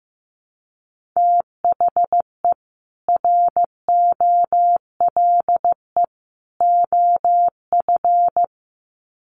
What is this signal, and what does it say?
Morse code, heard as a keyed tone, about 15 wpm, 700 Hz. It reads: THE ROLE OF
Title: Sets of 3 Words 15wpm